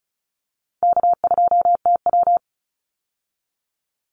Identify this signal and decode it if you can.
Morse code, heard as a keyed tone, about 34 wpm, 700 Hz. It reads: K2TW
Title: Call Signs - 35wpm